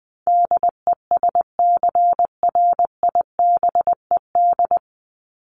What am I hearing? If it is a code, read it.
Morse code, heard as a keyed tone, about 20 wpm, 700 Hz. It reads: DESCRIBED
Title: ED Suffix - Top 100 Words with Base Words 20wpm